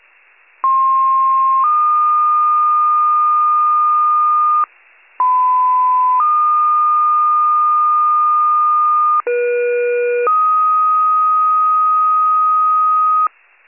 Two-tone page, heard as a one-second tone, then a three-second tone, about 1060 then 1220 Hz, about 1010 then 1220 Hz, about 480 then 1250 Hz. It is heard from 0.6 to 13.3 s.
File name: wcalert